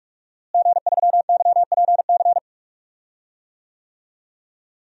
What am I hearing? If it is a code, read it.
Morse code, heard as a keyed tone, about 41 wpm, 700 Hz. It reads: G3YPC
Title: DXCC-100 Call Signs - 5 Characters - Repeated 2x - 45wpm